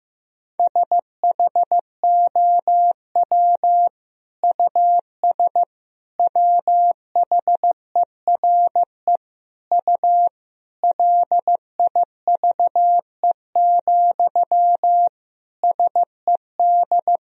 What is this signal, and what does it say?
Morse code, heard as a keyed tone, about 15 wpm, 700 Hz. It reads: SHOW US WHERE U LIVE, SED
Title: Sentences from Top 200 Words and Abbreviations 15wpm